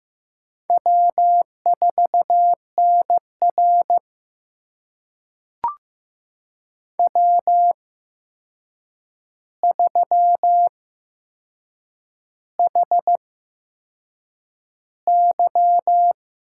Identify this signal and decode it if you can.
Morse code, heard as a keyed tone, about 15 wpm, spaced as for 6 wpm, 700 Hz. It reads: W4NR W3HY
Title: USA Call Signs - 4 Characters - 8x Character Spacing - 15wpm